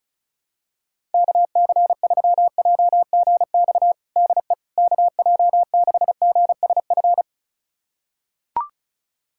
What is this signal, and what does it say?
Morse code, heard as a keyed tone, about 34 wpm, 700 Hz. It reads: KC3JGX DE KJ6GSF
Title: USA Callsign DE Callsign - 6 Characters - 35wpm